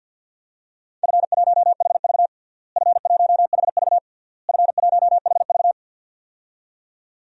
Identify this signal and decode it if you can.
Morse code, heard as a keyed tone, about 47 wpm, 700 Hz. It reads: F1HV F1HV F1HV
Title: DXCC-100 Call Signs - 4 Characters - Repeated 3x - 50wpm